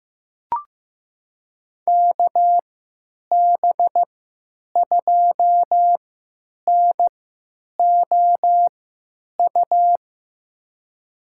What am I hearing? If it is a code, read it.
Morse code, heard as a keyed tone, about 15 wpm, spaced as for 9 wpm, 700 Hz. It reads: KB2NOU
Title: USA Call Signs - 3x Character Spacing - 15wpm